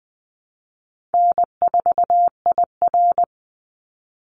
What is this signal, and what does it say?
Morse code, heard as a keyed tone, about 20 wpm, 700 Hz. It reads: N4IR
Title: Easy Call Signs - 20wpm